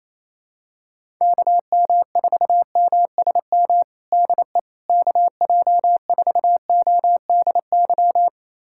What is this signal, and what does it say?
Morse code, heard as a keyed tone, about 28 wpm, 700 Hz. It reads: KM4MSM DE KJ4ODY
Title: USA Callsign DE Callsign 28wpm